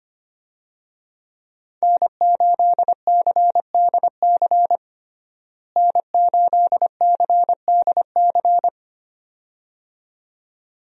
Morse code, heard as a keyed tone, about 25 wpm, 700 Hz. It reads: N8CDC N8CDC
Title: USA Call Signs - 5 Characters - Repeated 2x - 25wpm